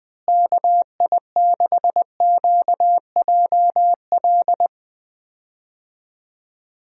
Morse code, heard as a keyed tone, about 20 wpm, 700 Hz. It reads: KI6QJL
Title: USA Call Signs - 6 Characters - Repeated 2x - 20wpm